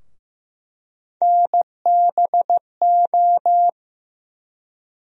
Morse code, heard as a keyed tone, about 15 wpm, 700 Hz. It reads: NBO
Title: USA Call Sign Postfixes - 3 Letters 15wpm